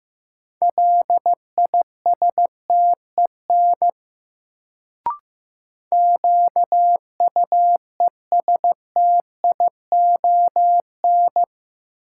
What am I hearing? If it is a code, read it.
Morse code, heard as a keyed tone, about 15 wpm, 700 Hz. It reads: LISTEN QUESTION
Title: Sentences from Top 400 Words - New Word Drill - 15wpm